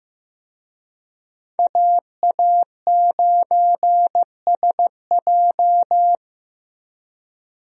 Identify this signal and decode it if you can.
Morse code, heard as a keyed tone, about 16 wpm, 700 Hz. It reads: AA9SJ
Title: USA Call Signs - 5 Characters - Repeated 3x - 15wpm